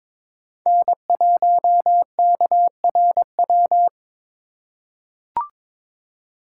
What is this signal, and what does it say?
Morse code, heard as a keyed tone, about 22 wpm, 700 Hz. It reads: N1KRW
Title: USA Call Signs - 5 Characters - 2x Character Spacing - 22wpm